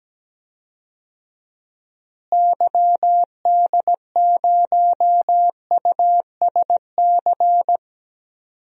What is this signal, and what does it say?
Morse code, heard as a keyed tone, about 17 wpm, 700 Hz. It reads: YD0USC
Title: DXCC-100 Call Signs - 6 Characters - Repeated 2x - 17wpm